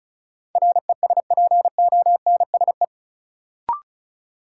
Morse code, heard as a keyed tone, about 36 wpm, 700 Hz. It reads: RESPONSE
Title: Sentences from Top 1400 Words - New Word Drill - 35wpm